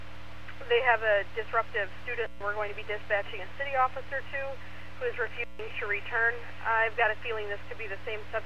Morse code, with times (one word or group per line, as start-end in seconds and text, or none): none